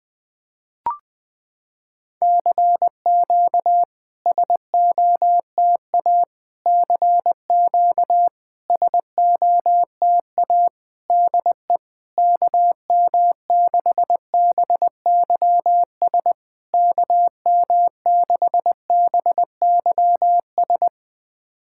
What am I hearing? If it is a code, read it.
Morse code, heard as a keyed tone, about 20 wpm, 700 Hz. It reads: CQ SOTA CQ SOTA DE KM6BYS KM6BYS
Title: SOTA QSO Practice 20wpm